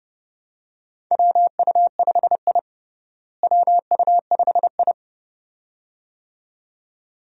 Morse code, heard as a keyed tone, about 30 wpm, 700 Hz. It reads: WU5I WU5I
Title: USA Call Signs - 4 Characters - Repeated 2x - 30wpm